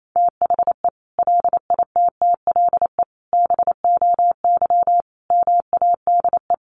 Morse code, heard as a keyed tone, about 28 wpm, 700 Hz. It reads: THE LITTLE BOY MADE
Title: Sentences from Top 1400 Words 28wpm